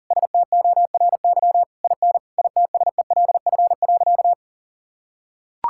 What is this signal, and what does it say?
Morse code, heard as a keyed tone, about 37 wpm, 700 Hz. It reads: STORY IN ITSELF.
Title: Sentences from Top 1000 Words - Review Part 5 - 40wpm